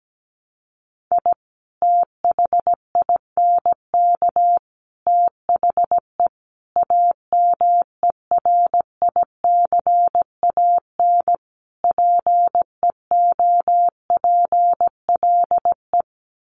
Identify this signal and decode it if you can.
Morse code, heard as a keyed tone, about 17 wpm, 700 Hz. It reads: I THINK THE AMERICAN PEOPLE
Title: Sets of 5 Words Encore 17wpm